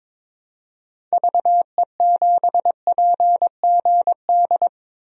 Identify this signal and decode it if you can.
Morse code, heard as a keyed tone, about 22 wpm, 700 Hz. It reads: VE7PGD
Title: DXCC-100 Call Signs - 6 Characters - 22wpm